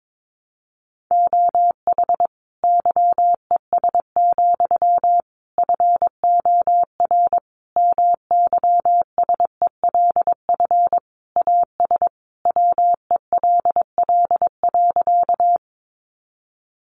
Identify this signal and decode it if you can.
Morse code, heard as a keyed tone, about 22 wpm, 700 Hz. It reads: OH YES, FOR MYSELF AS WELL.
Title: Sentences from Top 1300 Words 22wpm